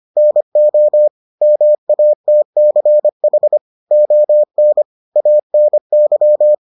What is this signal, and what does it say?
Morse code, heard as a keyed tone, about 25 wpm, 600 Hz. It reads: NO MATCH ON ANY